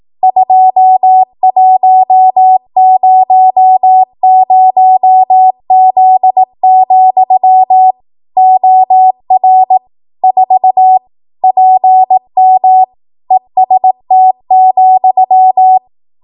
Morse code, heard as a keyed tone, about 18 wpm, 750 Hz. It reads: 2100Z, OR 4 PM EST,
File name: Archive / 18 WPM / 18WPM